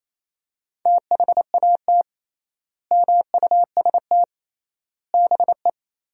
Morse code, heard as a keyed tone, about 28 wpm, 700 Hz. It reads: THAT MUST BE
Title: 3x Word-Spacing - Sentences from Top 400 Words - 28wpm